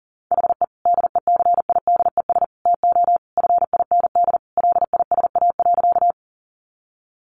Morse code, heard as a keyed tone, about 37 wpm, 700 Hz. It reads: HE DECIDES TO FIND LISA.
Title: Additional Easy Phrases 40wpm